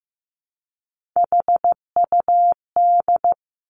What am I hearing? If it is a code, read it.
Morse code, heard as a keyed tone, about 15 wpm, 700 Hz. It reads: HUD
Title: Three Challenging Letter-Number Combinations 15wpm